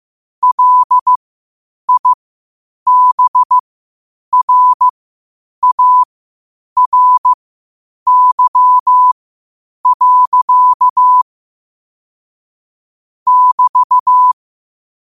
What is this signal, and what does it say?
Morse code, heard as a keyed tone, about 15 wpm, spaced as for 8 wpm, 1000 Hz. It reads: LIBRARY. <BT>
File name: qotdfiles / 8wpm